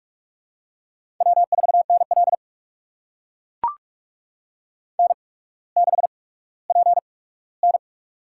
Morse code, heard as a keyed tone, about 44 wpm, spaced as for 19 wpm, 690 Hz. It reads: W4NL N6PN
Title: USA Call Signs - 4 Characters - 8x Character Spacing - 45wpm